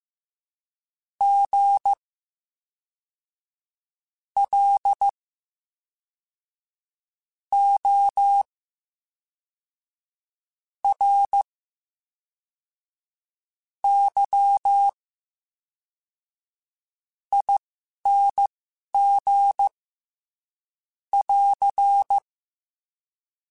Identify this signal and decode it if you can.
Morse code, heard as a keyed tone, about 15 wpm, spaced as for 4 wpm, 800 Hz. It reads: GLORYING <AR>
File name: Special INGs / Special INGs-mp3s / VerySlowToFaster Ing @10